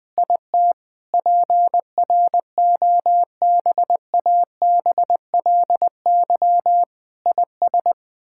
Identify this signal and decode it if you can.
Morse code, heard as a keyed tone, about 20 wpm, 700 Hz. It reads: IT PROBABLY IS
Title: Sets of 9 Words - 20wpm